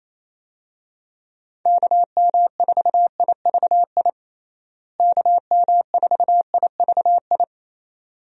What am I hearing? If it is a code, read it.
Morse code, heard as a keyed tone, about 27 wpm, 700 Hz. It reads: KM4IVI KM4IVI
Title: USA Call Signs - 6 Characters - Repeated 2x - 28wpm